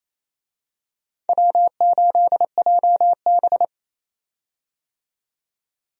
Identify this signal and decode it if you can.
Morse code, heard as a keyed tone, about 27 wpm, 700 Hz. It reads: W8JB
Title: USA Call Signs - 4 Characters - 28wpm